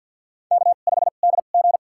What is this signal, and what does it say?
Morse code, heard as a keyed tone, about 47 wpm, spaced as for 35 wpm, 700 Hz. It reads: K5DG
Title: USA Call Signs - 4 Characters - 2x Character Spacing - 50wpm